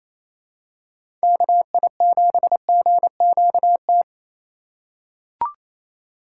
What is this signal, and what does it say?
Morse code, heard as a keyed tone, about 28 wpm, 700 Hz. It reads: KI7GQT
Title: USA Call Signs - 2x Character Spacing - 28wpm